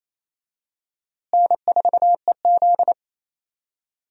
Morse code, heard as a keyed tone, about 28 wpm, 700 Hz. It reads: N4EZ
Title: USA Call Signs - 4 Characters - Repeated 3x - 28wpm